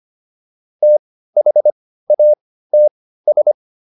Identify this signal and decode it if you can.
Morse code, heard as a keyed tone, about 25 wpm, spaced as for 15 wpm, 600 Hz. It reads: THATS